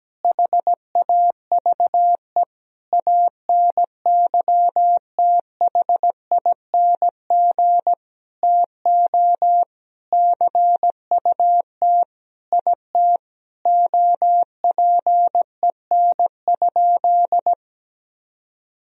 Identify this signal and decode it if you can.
Morse code, heard as a keyed tone, about 17 wpm, 700 Hz. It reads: HAVE ANYTHING TO CUT IT OPEN?
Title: Sentences from Top 1000 Words - Review Part 1 - 17wpm